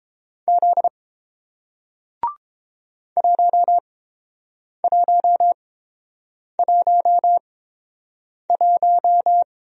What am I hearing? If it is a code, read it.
Morse code, heard as a keyed tone, about 30 wpm, spaced as for 7 wpm, 700 Hz. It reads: Z 1111